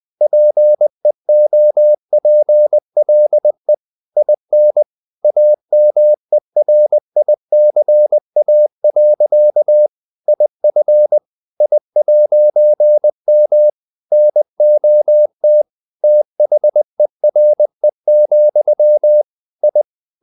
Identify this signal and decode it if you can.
Morse code, heard as a keyed tone, about 20 wpm, 600 Hz. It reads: PEOPLE IN AMERICA. IF I'M NOT THERE, I
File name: CW / Practice / EnglishQuotations0320